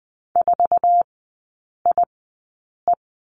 Morse code, heard as a keyed tone, about 20 wpm, spaced as for 8 wpm, 700 Hz. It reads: 4IE
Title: 074 Lesson 13e - Letter L - Three Characters 20wpm